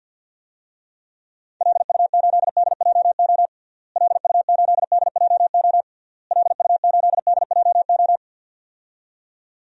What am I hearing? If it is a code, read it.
Morse code, heard as a keyed tone, about 52 wpm, 700 Hz. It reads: LU8DJO LU8DJO LU8DJO
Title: DXCC-100 Call Signs - 6 Characters - Repeated 3x - 50wpm